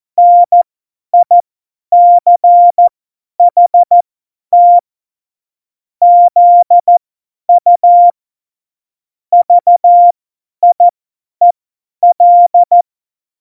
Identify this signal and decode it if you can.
Morse code, heard as a keyed tone, about 14 wpm, spaced as for 10 wpm, 700 Hz. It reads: NICHT ZU VIEL
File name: Morsekurs / Witze / Witz 11 Juli 17, Tempo 50 F70